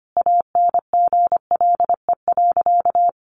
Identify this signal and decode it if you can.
Morse code, heard as a keyed tone, about 25 wpm, 700 Hz. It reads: ANGLE.
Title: Sentences from Top 1500 Words 25wpm